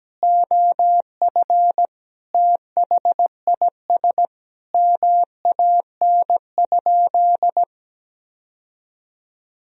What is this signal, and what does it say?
Morse code, heard as a keyed tone, about 17 wpm, 700 Hz. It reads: OF THIS MAN?
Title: Sentences from Top 1000 Words - Review Part 3 - 17wpm